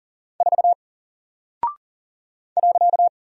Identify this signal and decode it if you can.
Morse code, heard as a keyed tone, about 40 wpm, 700 Hz. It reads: 4 .
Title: Single Character (Letter:Number:Punctuation:Prosign) Alt 40wpm